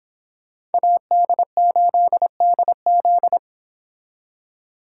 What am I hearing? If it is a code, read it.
Morse code, heard as a keyed tone, about 31 wpm, 700 Hz. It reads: AD8DZ